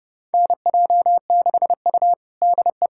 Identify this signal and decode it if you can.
Morse code, heard as a keyed tone, about 30 wpm, 700 Hz. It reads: NJ6U DE
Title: USA Callsign DE Callsign - 4 Characters - 30wpm